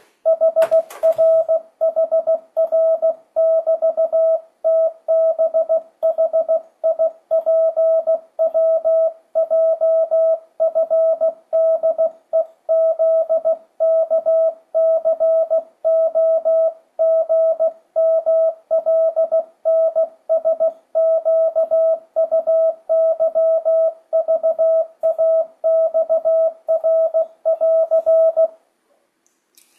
Morse code, heard as a keyed tone, about 16 wpm, 650 Hz. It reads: HRHR<BT>TBHIPWJFDEZKCOGMLNSQUYVAXR<AR>
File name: 2025cw / speedup / 16wpm